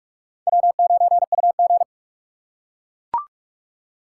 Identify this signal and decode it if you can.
Morse code, heard as a keyed tone, about 43 wpm, 700 Hz. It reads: W9UG